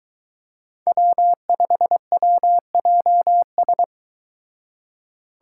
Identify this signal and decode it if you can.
Morse code, heard as a keyed tone, about 23 wpm, 700 Hz. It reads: W5WJS